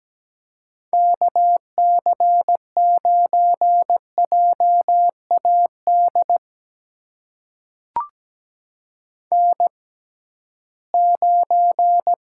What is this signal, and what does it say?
Morse code, heard as a keyed tone, about 17 wpm, spaced as for 13 wpm, 700 Hz. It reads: KC9JAD N9
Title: USA Call Signs - 6x Character Spacing - 17wpm